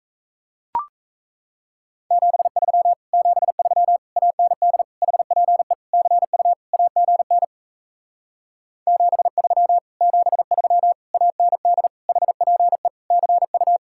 Morse code, heard as a keyed tone, about 37 wpm, 700 Hz. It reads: 73 73 AND HPE CU AGN 73 73 AND HPE CU